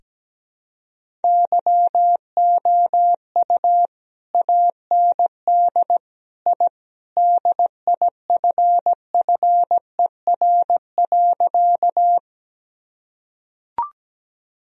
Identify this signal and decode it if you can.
Morse code, heard as a keyed tone, about 17 wpm, 700 Hz. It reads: YOU AND I DIFFER.
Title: Sentences from Top 1000 Words - Review Part 1 - 17wpm